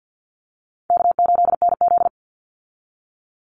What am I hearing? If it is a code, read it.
Morse code, heard as a keyed tone, about 50 wpm, 700 Hz. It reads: K8NZ
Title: Easy Call Signs - 50wpm